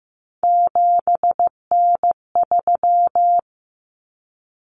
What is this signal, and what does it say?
Morse code, heard as a keyed tone, about 15 wpm, 700 Hz. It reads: 7N3
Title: Common Cut Numbers - 3-Digit 15wpm